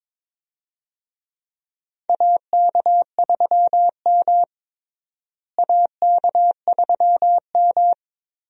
Morse code, heard as a keyed tone, about 22 wpm, 700 Hz. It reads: AK3M AK3M
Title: USA Call Signs - 4 Characters - Repeated 2x - 22wpm